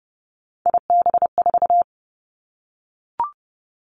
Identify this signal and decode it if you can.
Morse code, heard as a keyed tone, about 30 wpm, 700 Hz. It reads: IB4
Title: Three Challenging Letter-Number Combinations 30wpm